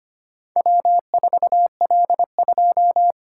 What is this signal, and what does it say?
Morse code, heard as a keyed tone, about 25 wpm, 700 Hz. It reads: W4L2
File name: practice / 25wpm / Four Letter-Number Combinations - 25wpm